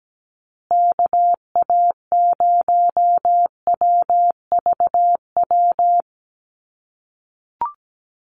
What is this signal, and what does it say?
Morse code, heard as a keyed tone, about 17 wpm, 700 Hz. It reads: KA0WVW
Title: USA Call Signs Repeated 3x 17wpm